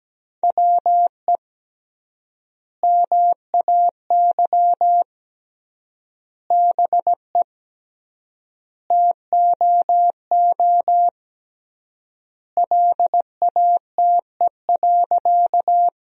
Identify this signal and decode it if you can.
Morse code, heard as a keyed tone, about 17 wpm, 700 Hz. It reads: WE MAY BE TOO LATE.
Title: 3x Word-Spacing - Sentences from Top 300 Words - 17wpm